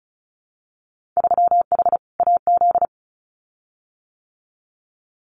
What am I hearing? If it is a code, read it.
Morse code, heard as a keyed tone, about 34 wpm, 700 Hz. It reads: 3H AZ
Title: Winter Field Day Exchanges Encore 35wpm